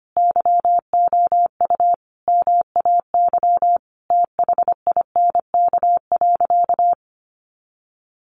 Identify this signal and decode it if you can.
Morse code, heard as a keyed tone, about 25 wpm, 700 Hz. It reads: YOU MAY THINK.
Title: Sentences from Top 1100 Words 25wpm